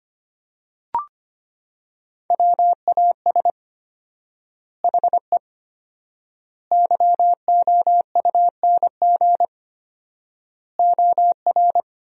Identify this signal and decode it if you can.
Morse code, heard as a keyed tone, about 25 wpm, 700 Hz. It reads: WAS HE YOUNG OR
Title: 4x Word-Spacing - Sentences from Top 400 Words - 25wpm